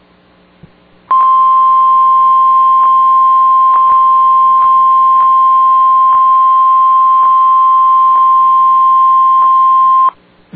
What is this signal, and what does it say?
Emergency Alert System (SAME) tones (the 1050 Hz attention tone). It is heard from 1.1 to 10.1 s.